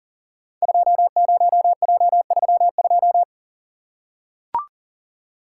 Morse code, heard as a keyed tone, about 42 wpm, 700 Hz. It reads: 20J32
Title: Code Groups - 5 Letter:Numbers - 40wpm